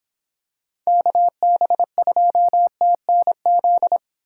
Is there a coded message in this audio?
Yes: Morse code from 0.9 to 4.0 s, heard as a keyed tone, about 26 wpm, 700 Hz. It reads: KB2TNZ